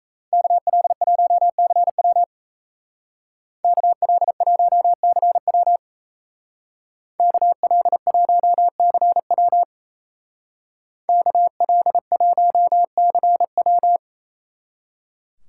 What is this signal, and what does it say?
Morse code, heard as a keyed tone, about 34 wpm, 700 Hz. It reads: KL1CW KL1CW KL1CW KL1CW
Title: Speed Racing - USA Call Signs 28wpm